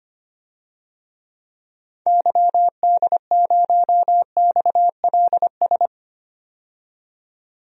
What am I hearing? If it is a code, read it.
Morse code, heard as a keyed tone, about 25 wpm, 690 Hz. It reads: YD0XLS
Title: DXCC-100 Call Signs - 6 Characters - 25wpm